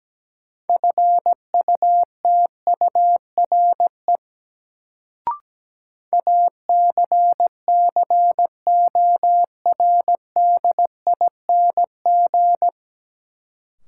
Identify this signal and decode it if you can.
Morse code, heard as a keyed tone, about 17 wpm, 700 Hz. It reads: FUTURE ACCORDING
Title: Sentences from Top 700 Words - New Word Drill - 17wpm